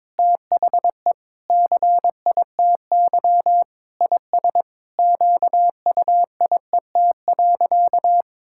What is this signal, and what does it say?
Morse code, heard as a keyed tone, about 21 wpm, 700 Hz. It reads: THE CITY IS QUIET.
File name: Sentences with 7-Word Phrases - 22wpm